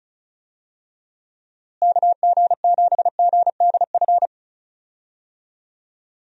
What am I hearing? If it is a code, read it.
Morse code, heard as a keyed tone, about 34 wpm, 700 Hz. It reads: KG7GDF
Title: USA Call Signs - 6 Characters - Repeated 3x - 35wpm